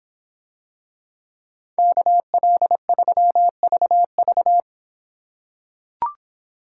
Morse code, heard as a keyed tone, about 24 wpm, 700 Hz. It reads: KL3VV